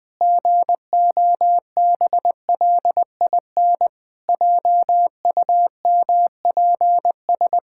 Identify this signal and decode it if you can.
Morse code, heard as a keyed tone, about 20 wpm, 700 Hz. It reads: GOBLIN JUMPS
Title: Pangram Challenge v2 - 20wpm